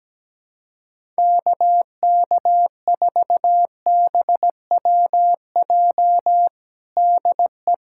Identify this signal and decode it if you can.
Morse code, heard as a keyed tone, about 17 wpm, 700 Hz. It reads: KK4BWJ DE
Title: USA Callsign DE Callsign - 6 Characters - 17wpm